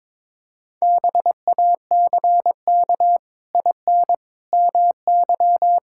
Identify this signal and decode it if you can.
Morse code, heard as a keyed tone, about 22 wpm, 700 Hz. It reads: BACK IN MY
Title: Pangram Challenge v2 - 22wpm